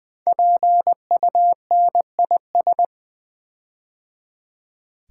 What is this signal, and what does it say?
Morse code, heard as a keyed tone, about 20 wpm, 700 Hz. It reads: PUNIS
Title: Build A Word - 5 Letters At A Time - 10-Letter Words - 20wpm